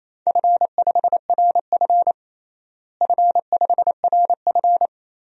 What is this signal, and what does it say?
Morse code, heard as a keyed tone, about 28 wpm, 700 Hz. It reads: F5RF F5RF
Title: DXCC-100 Call Signs - 4 Characters - Repeated 2x - 28wpm